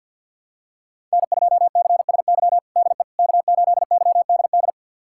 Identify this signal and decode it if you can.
Morse code, heard as a keyed tone, about 51 wpm, 700 Hz. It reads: N2CSY DE K8YDD
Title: USA Callsign DE Callsign - 5 Characters - 50wpm